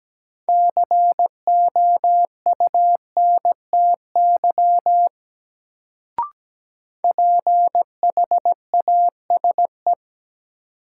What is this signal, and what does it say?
Morse code, heard as a keyed tone, about 17 wpm, 700 Hz. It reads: COUNTY PHASE
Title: Sentences from Top 1200 Words - New Word Drill Alt - 17wpm